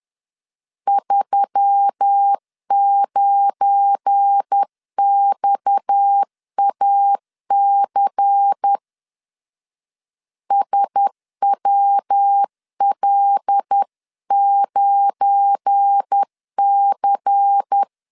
Morse code, heard as a keyed tone, about 10 wpm, 800 Hz. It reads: 39XAC SWL9C
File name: LEZIONI / 7 GF9Z / 7 GF9 10wpm 2747